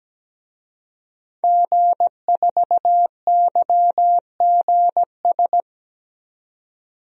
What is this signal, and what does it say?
Morse code, heard as a keyed tone, about 17 wpm, 700 Hz. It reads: G4YGS
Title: DXCC-100 Call Signs - 5 Characters - 17wpm